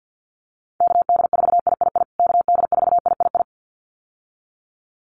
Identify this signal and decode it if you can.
Morse code, heard as a keyed tone, about 50 wpm, 700 Hz. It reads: KD4III KD4III
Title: USA Call Signs Repeated 2x 50wpm